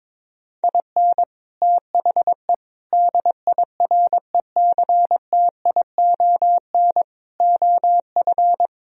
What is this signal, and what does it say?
Morse code, heard as a keyed tone, about 23 wpm, 700 Hz. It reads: IN THE DIRECTION OF
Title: Speed-Racing - Sets of 4 Words 22wpm